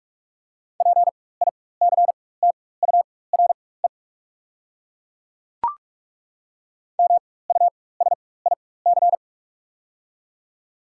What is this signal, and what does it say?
Morse code, heard as a keyed tone, about 42 wpm, spaced as for 20 wpm, 700 Hz. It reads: PICTURE MUSIC
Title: 4x Character Spacing - Top 300 Words - 45wpm